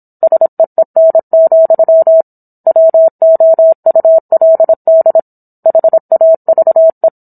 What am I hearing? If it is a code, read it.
Morse code, heard as a keyed tone, about 27 wpm, 650 Hz. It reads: SEEN, WOULD HAVE
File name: Book One / 17-Chapter 17 - 26 WPM